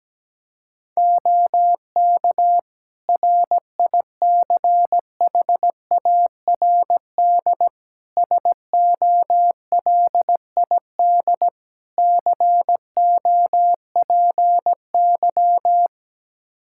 Morse code, heard as a keyed tone, about 17 wpm, 700 Hz. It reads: OK RICHARD SOLID COPY
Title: Practice QSO - Exchange 3 - More Info - 17wpm